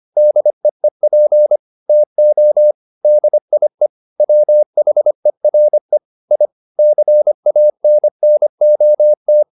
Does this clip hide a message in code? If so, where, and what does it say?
Yes: Morse code from 0.2 to 9.4 s, heard as a keyed tone, about 25 wpm, 600 Hz. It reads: DEEP TO DIE WHERE I CANNOT